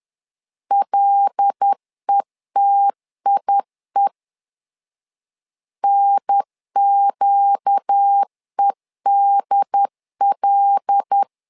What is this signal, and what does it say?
Morse code, heard as a keyed tone, about 10 wpm, 800 Hz. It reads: LETIE NQEDL
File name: LEZIONI / 5 LBVC / 5 L 10wpm 2259